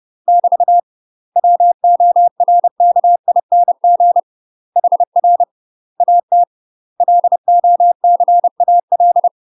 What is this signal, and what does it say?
Morse code, heard as a keyed tone, about 30 wpm, 700 Hz. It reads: <BT> WORKING HR AT LOCAL
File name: UK MASTER QSOs TRAINING / 30WPM / UK MASTER QSOs 0005